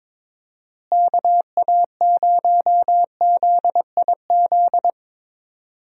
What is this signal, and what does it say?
Morse code, heard as a keyed tone, about 22 wpm, 700 Hz. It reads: KA0ZIZ
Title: USA Call Signs - 6 Characters - Repeated 3x - 22wpm